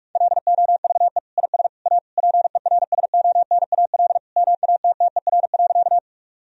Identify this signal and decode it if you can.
Morse code, heard as a keyed tone, about 46 wpm, 700 Hz. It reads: LOVE IS A PERSONAL MATTER.
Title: Sentences from Top 1000 Words - Review Part 4 - 45wpm